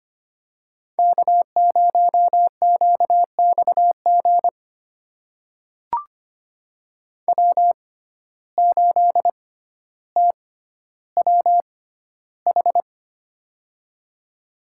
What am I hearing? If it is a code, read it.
Morse code, heard as a keyed tone, about 25 wpm, spaced as for 13 wpm, 700 Hz. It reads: K0QXG W8TWH
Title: USA Call Signs - 5 Characters - 6x Character Spacing - 25wpm